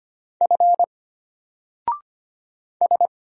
Morse code, heard as a keyed tone, about 25 wpm, 700 Hz. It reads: F S
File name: Single-Letter-Number-25wpm